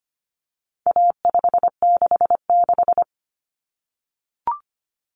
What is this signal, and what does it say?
Morse code, heard as a keyed tone, about 25 wpm, 700 Hz. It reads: A566
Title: Common Cut Numbers - 4-Digit 25wpm